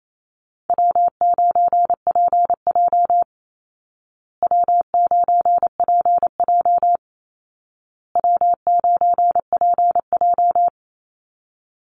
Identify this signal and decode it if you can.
Morse code, heard as a keyed tone, about 28 wpm, 700 Hz. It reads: W9PJ W9PJ W9PJ
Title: Easy Call Signs Repeated 3x - 28wpm